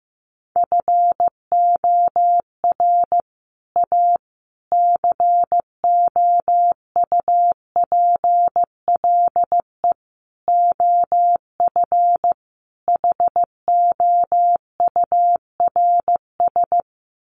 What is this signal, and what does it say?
Morse code, heard as a keyed tone, about 15 wpm, 700 Hz. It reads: FOR A COUPLE OF HOURS
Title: Sets of 5 Words Encore2 15wpm